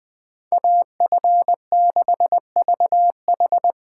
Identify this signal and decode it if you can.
Morse code, heard as a keyed tone, about 20 wpm, 700 Hz. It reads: AF6VH